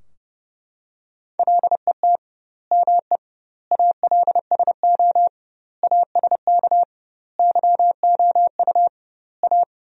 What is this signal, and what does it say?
Morse code, heard as a keyed tone, about 30 wpm, 700 Hz. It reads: LET ME ALSO ASK YOU A
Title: 2x Word-Spacing - Sentences from Top 400 Words - 30wpm